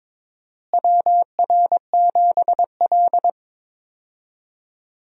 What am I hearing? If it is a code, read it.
Morse code, heard as a keyed tone, about 21 wpm, 700 Hz. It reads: WR7L
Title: USA Call Signs - 4 Characters - Repeated 3x - 22wpm